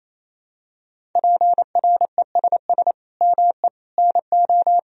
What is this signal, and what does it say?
Morse code, heard as a keyed tone, about 27 wpm, 700 Hz. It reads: PRESS ME NO
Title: Sentences from Top 1000 Words - Review Part 1 - 28wpm